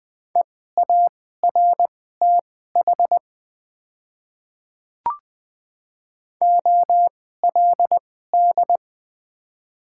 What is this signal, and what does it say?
Morse code, heard as a keyed tone, about 20 wpm, spaced as for 14 wpm, 700 Hz. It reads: EARTH OLD
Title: 2x Character Spacing - Top 200 Words - 20wpm